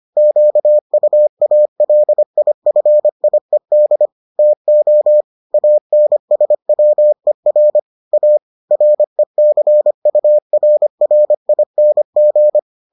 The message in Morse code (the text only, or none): QUALIFIED TO ANSWER A RECURRING